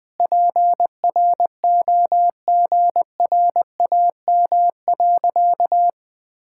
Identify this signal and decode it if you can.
Morse code, heard as a keyed tone, about 20 wpm, 700 Hz. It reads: PROGRAM.
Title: Sentences from Top 1000 Words - Review Part 5 - 20wpm